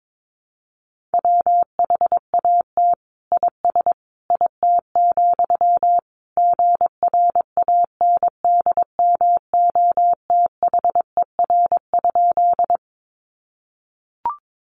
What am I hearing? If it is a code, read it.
Morse code, heard as a keyed tone, about 22 wpm, 700 Hz. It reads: WHAT IS IT, GRANDMOTHER?
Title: Sentences from Top 1200 Words 22wpm